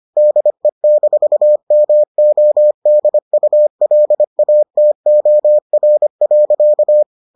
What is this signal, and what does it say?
Morse code, heard as a keyed tone, about 25 wpm, 600 Hz. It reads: DE-MODULATOR.